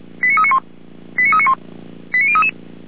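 Five-tone selective call (ZVEI), heard as quick runs of five tones, tones of about 70 ms. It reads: 89391 89391 89033
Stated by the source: ZVEI five-tone call received by SDR.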